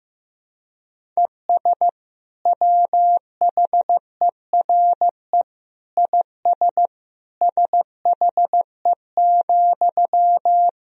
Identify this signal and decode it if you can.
Morse code, heard as a keyed tone, about 15 wpm, 700 Hz. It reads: ES WHERE IS SHE,
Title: Sentences from Top 200 Words and Abbreviations 15wpm